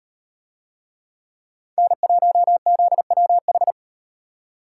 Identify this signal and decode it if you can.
Morse code, heard as a keyed tone, about 38 wpm, 700 Hz. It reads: N1ZWH